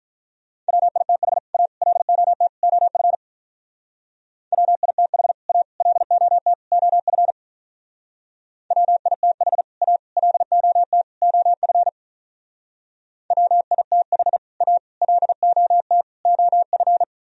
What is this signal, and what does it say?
Morse code, heard as a keyed tone, about 46 wpm, 700 Hz. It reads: WITH A LOT OF WITH A LOT OF WITH A LOT OF WITH A LOT OF